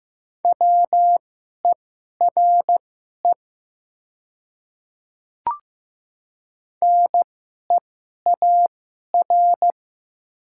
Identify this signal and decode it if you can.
Morse code, heard as a keyed tone, about 15 wpm, spaced as for 11 wpm, 700 Hz. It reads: WERE NEAR
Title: 2x Character Spacing - Top 200 Words - 15wpm